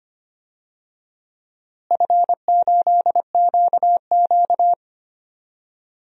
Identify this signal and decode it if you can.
Morse code, heard as a keyed tone, about 25 wpm, 700 Hz. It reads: F8QQ
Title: DXCC-100 Call Signs - 4 Characters - Repeated 2x - 25wpm